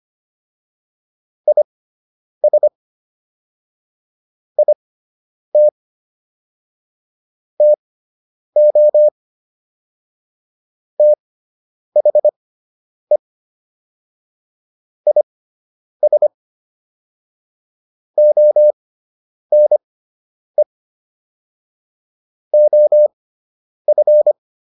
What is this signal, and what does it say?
Morse code, heard as a keyed tone, about 25 wpm, spaced as for 9 wpm, 600 Hz. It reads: IS IT TO THE IS ONE OF